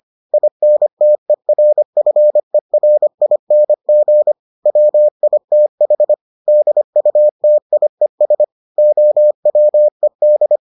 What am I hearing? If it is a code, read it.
Morse code, heard as a keyed tone, about 25 wpm, 600 Hz. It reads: INTERFERING WITH DUTIES OWED